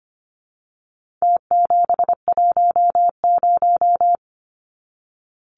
Morse code, heard as a keyed tone, about 25 wpm, 700 Hz. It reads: T710
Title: Numbers - Contest Style - 4-Digit 25wpm